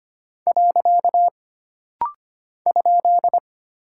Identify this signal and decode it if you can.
Morse code, heard as a keyed tone, about 25 wpm, 700 Hz. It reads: . ?
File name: Single Punctuation-Prosign - Rapid-Fire - 25wpm